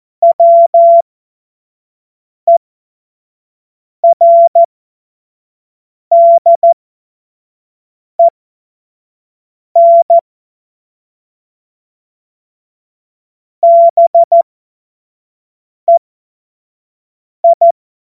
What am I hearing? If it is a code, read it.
Morse code, heard as a keyed tone, about 14 wpm, spaced as for 5 wpm, 680 Hz. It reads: WERDEN BEI
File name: Morsekurs / Witze / Witz 07. Dezember 2021, 25 F70